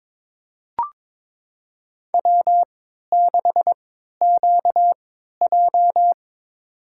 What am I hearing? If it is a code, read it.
Morse code, heard as a keyed tone, about 22 wpm, spaced as for 13 wpm, 700 Hz. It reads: W6QJ
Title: USA Call Signs - 4 Characters - 3x Character Spacing - 22wpm